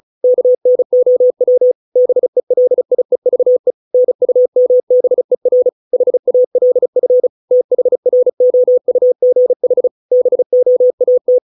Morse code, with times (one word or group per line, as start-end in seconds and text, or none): 0.2-1.7 KNOW
2.0-3.7 BELIEVE
3.9-5.7 NUMBER
5.9-7.3 HALF
7.5-9.9 THROUGH
10.1-11.4 BOAT